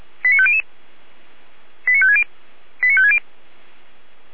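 Five-tone selective call (ZVEI), heard as quick runs of five tones, tones of about 70 ms. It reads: 89500 89560 89560
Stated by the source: ZVEI five-tone call received by SDR, fire brigade feuerwehr-orlamuende (F85360).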